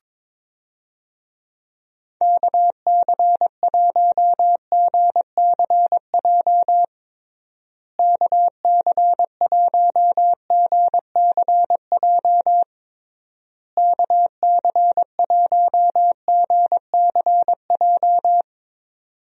Morse code, heard as a keyed tone, about 22 wpm, 700 Hz. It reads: KC1GCJ KC1GCJ KC1GCJ
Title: USA Call Signs - 6 Characters - Repeated 3x - 22wpm